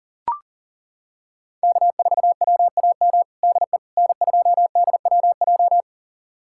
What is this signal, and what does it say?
Morse code, heard as a keyed tone, about 37 wpm, 700 Hz. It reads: K4WAM DE N2DWJ
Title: USA Callsign DE Callsign - 5 Characters - 40wpm